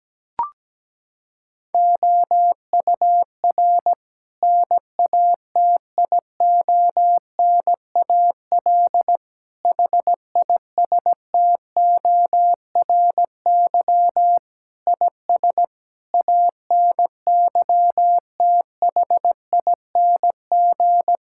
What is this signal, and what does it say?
Morse code, heard as a keyed tone, about 17 wpm, 700 Hz. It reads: OUR NATIONAL HISTORY IS ANYTHING
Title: Sentences from Top 1000 Words - Review Part 6 - 17wpm